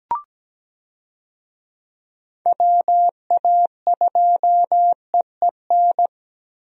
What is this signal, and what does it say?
Morse code, heard as a keyed tone, about 17 wpm, 700 Hz. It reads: WA2EEN
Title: USA Call Signs - 6 Characters - Repeated 3x - 17wpm